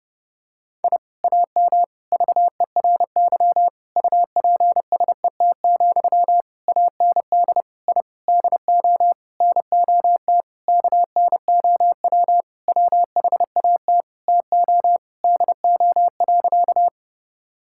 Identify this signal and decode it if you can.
Morse code, heard as a keyed tone, about 30 wpm, 700 Hz. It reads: I AM VERY UPSET, AND I DO NOT KNOW WHAT TO DO.
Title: Sentences from Top 1000 Words - Review Part 5 - 30wpm